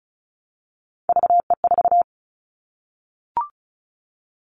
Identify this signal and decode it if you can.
Morse code, heard as a keyed tone, about 36 wpm, 700 Hz. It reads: VE4
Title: Callsign Prefixes for Top 100 DXCC - 35wpm